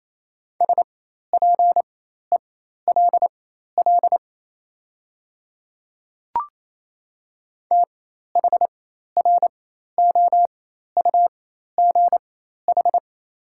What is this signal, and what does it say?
Morse code, heard as a keyed tone, about 28 wpm, spaced as for 13 wpm, 700 Hz. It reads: SPELL THROUGH
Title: 4x Character Spacing - Top 200 Words - 28wpm